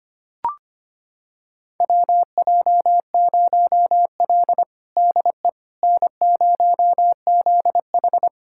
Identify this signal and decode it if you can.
Morse code, heard as a keyed tone, about 25 wpm, 700 Hz. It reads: WJ0L DE N0ZH
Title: USA Callsign DE Callsign - 4 Characters - 25wpm